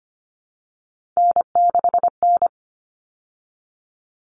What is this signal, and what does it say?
Morse code, heard as a keyed tone, about 25 wpm, 700 Hz. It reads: N6N
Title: All Cut Numbers - 3-Digit 25wpm